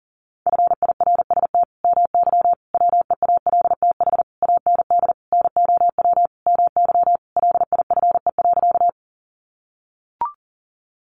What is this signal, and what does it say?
Morse code, heard as a keyed tone, about 40 wpm, 700 Hz. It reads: FIRST MY WEALTH AND NOW MY LIFE.
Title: Sentences from Top 1200 Words 40wpm